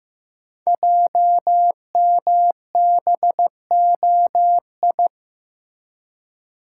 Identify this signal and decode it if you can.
Morse code, heard as a keyed tone, about 15 wpm, 700 Hz. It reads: JMBOI
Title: Code Groups - 5 Letters - 15wpm